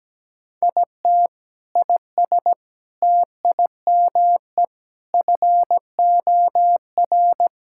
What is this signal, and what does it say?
Morse code, heard as a keyed tone, about 17 wpm, 700 Hz. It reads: IT IS TIME FOR
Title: Sentences from Top 1000 Words - Review Part 6 - 17wpm